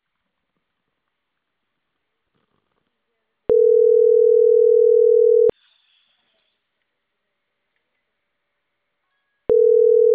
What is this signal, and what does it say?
Ringback tone (North America), 440 + 480 Hz.